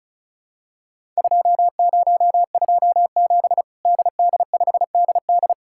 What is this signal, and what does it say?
Morse code, heard as a keyed tone, about 36 wpm, 700 Hz. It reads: 2027 DD5DD
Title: Worked All Europe DX Contest - QTC Exchange - 35wpm